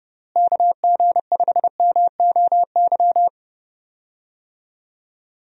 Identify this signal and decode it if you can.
Morse code, heard as a keyed tone, about 31 wpm, 700 Hz. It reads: KG5MOY
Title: USA Call Signs - 6 Characters - Repeated 2x - 30wpm